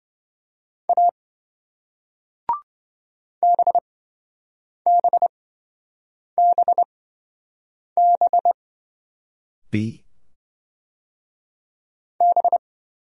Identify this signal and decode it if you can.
Morse code, heard as a keyed tone, about 28 wpm, spaced as for 7 wpm, 700 Hz. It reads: A BBBB B